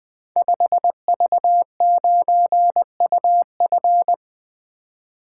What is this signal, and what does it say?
Morse code, heard as a keyed tone, about 21 wpm, 700 Hz. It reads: 5V9UF
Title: Code Groups - 5 Letter:Numbers - 20wpm